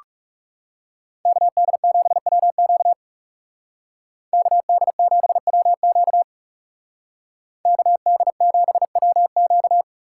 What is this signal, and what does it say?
Morse code, heard as a keyed tone, about 39 wpm, 700 Hz. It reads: KD7WQ KD7WQ KD7WQ